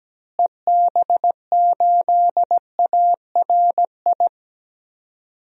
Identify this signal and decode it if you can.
Morse code, heard as a keyed tone, about 17 wpm, 700 Hz. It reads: EB8ARI
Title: DXCC-100 Call Signs - 6 Characters - Repeated 3x - 17wpm